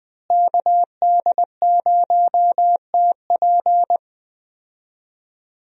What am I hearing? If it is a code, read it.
Morse code, heard as a keyed tone, about 20 wpm, 700 Hz. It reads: KD0TP
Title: USA Call Signs - 5 Characters - Repeated 3x - 20wpm